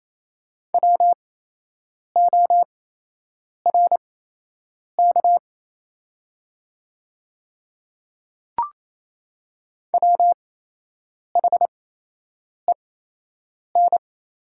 Morse code, heard as a keyed tone, about 27 wpm, spaced as for 8 wpm, 700 Hz. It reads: WORK WHEN
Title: 8x Character Spacing - Top 500 Words - 28wpm